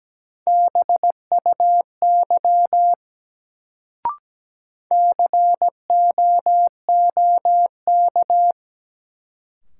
Morse code, heard as a keyed tone, about 17 wpm, 700 Hz. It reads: BUY COOK
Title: Sentences from Top 700 Words - New Word Drill - 17wpm